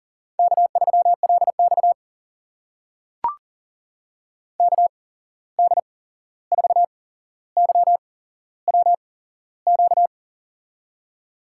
Morse code, heard as a keyed tone, about 40 wpm, spaced as for 15 wpm, 700 Hz. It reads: K3LX KD4YWQ
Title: USA Call Signs - 8x Character Spacing - 40wpm